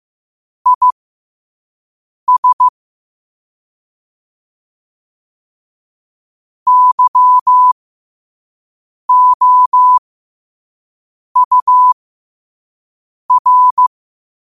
Morse code, heard as a keyed tone, about 15 wpm, spaced as for 5 wpm, 1000 Hz. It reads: IS YOUR
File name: qotdfiles / 5wpm